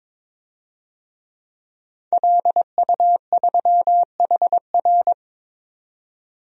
Morse code, heard as a keyed tone, about 22 wpm, 700 Hz. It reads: LU3HR
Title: DXCC-100 Call Signs - 5 Characters - 22wpm